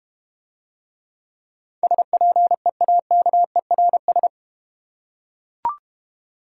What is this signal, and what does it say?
Morse code, heard as a keyed tone, about 32 wpm, 700 Hz. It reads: SPEAKERS